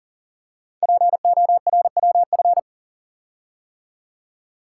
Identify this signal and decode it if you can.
Morse code, heard as a keyed tone, about 38 wpm, 700 Hz. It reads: PORWF
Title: Code Groups - 5 Letters - 40wpm